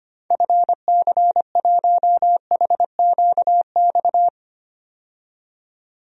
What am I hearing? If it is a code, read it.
Morse code, heard as a keyed tone, about 25 wpm, 700 Hz. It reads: FC1HQX
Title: DXCC-100 Call Signs - 6 Characters - 25wpm